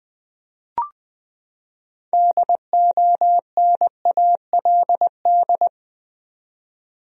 Morse code, heard as a keyed tone, about 21 wpm, 700 Hz. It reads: DONALD
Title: USA Common Names - 20wpm